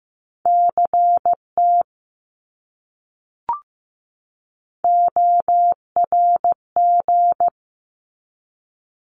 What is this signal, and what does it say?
Morse code, heard as a keyed tone, about 15 wpm, 700 Hz. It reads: CT ORG
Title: ARRL Sections 15wpm